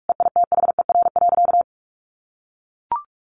Morse code, heard as a keyed tone, about 44 wpm, 700 Hz. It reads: EITHER.
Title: Sentences from Top 600 Words 45wpm